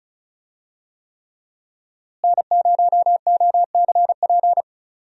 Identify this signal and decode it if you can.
Morse code, heard as a keyed tone, about 34 wpm, 700 Hz. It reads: N0OCP
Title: USA Call Signs - 5 Characters - Repeated 3x - 35wpm